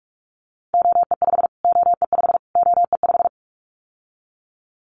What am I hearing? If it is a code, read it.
Morse code, heard as a keyed tone, about 44 wpm, 700 Hz. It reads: OE5 OE5 OE5
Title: Callsign Prefixes for Top 25 DXCC - Repeated 3x - 45wpm